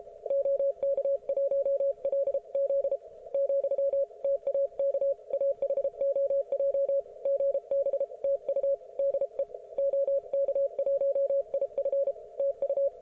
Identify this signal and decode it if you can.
Morse code, heard as a keyed tone, about 33 wpm, 550 Hz. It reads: OK1LZ , TAK AHOJ GB TU DE OK1IF TU
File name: Archiv / CUC157 / ok1fhi cul157